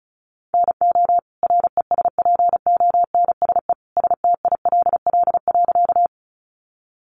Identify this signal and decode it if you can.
Morse code, heard as a keyed tone, about 34 wpm, 700 Hz. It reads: NO RESPONSE STILL.
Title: Sentences from Top 1400 Words 35wpm